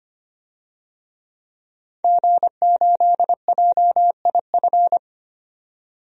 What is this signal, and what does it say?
Morse code, heard as a keyed tone, about 25 wpm, 700 Hz. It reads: G8JIF
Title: DXCC-100 Call Signs - 5 Characters - 25wpm